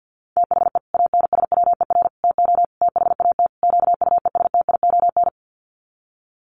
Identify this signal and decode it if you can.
Morse code, heard as a keyed tone, about 50 wpm, 700 Hz. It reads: THE ANSWER TO THAT QUESTION
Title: Sets of 5 Words Encore2 50wpm